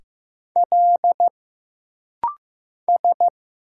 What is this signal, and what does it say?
Morse code, heard as a keyed tone, about 15 wpm, 700 Hz. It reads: L S
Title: Single Letter-Number Alt - Rapid-Fire 15wpm